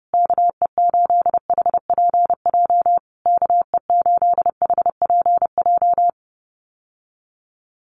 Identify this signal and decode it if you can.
Morse code, heard as a keyed tone, about 30 wpm, 700 Hz. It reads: KE8HPJ KE8HPJ
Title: USA Call Signs Repeated 3x 30wpm